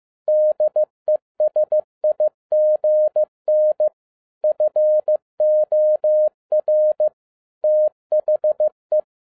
Morse code, heard as a keyed tone, about 15 wpm, 610 Hz. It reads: DESIGN FOR THE
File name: 2011-10-mp3 / n02-FIELD DAY CONTEST QRV FROM SAKAE VILLAGE 15wpm